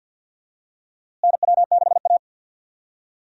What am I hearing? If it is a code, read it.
Morse code, heard as a keyed tone, about 52 wpm, 700 Hz. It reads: NW6A
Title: USA Call Signs - 4 Characters - Repeated 2x - 50wpm